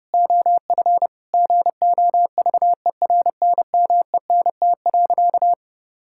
Morse code, heard as a keyed tone, about 30 wpm, 700 Hz. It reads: OF GOVERNMENT.
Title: Sentences from Top 1000 Words - Review Part 5 - 30wpm